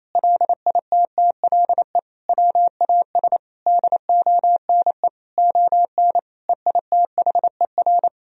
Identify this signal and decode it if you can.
Morse code, heard as a keyed tone, about 28 wpm, 700 Hz. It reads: LITTLE WAS DONE ON EITHER
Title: Sentences from Top 1000 Words - Review Part 4 - 28wpm